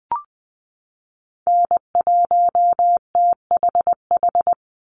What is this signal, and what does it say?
Morse code, heard as a keyed tone, about 20 wpm, 700 Hz. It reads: N1THH
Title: USA Call Signs 20wpm